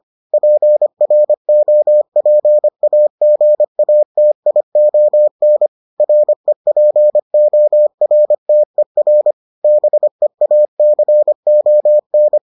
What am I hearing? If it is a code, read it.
Morse code, heard as a keyed tone, about 25 wpm, 600 Hz. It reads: PROPAGATION REPORTER BEACON